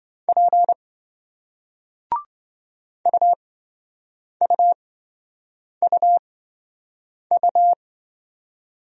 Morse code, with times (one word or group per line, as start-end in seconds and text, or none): 0.3-0.7 P
3.1-7.7 UUUU